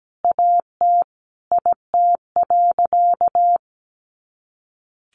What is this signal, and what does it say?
Morse code, heard as a keyed tone, about 17 wpm, 700 Hz. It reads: AT IT.
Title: Additional Easy Phrases 17wpm